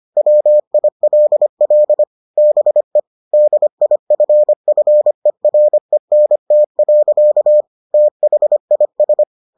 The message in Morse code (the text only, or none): WILL BE DIFFERENT. THIS